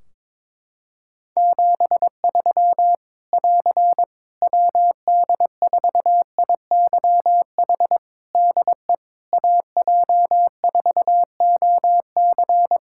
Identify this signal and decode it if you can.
Morse code, heard as a keyed tone, about 22 wpm, 700 Hz. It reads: 73 <AR> WD4IYH DE AJ4OC
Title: Practice QSO - Exchange 4 - Ending - 22wpm